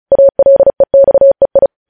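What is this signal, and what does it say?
Morse code, heard as a keyed tone, about 36 wpm, 550 Hz. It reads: ALEXEI
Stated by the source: News headlines sent in Morse at 35 wpm.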